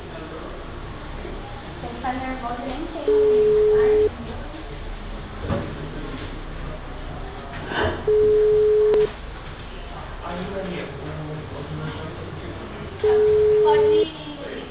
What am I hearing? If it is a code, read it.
Ringback tone (Europe, 425 Hz), with speech.
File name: out- -2002- - - .7368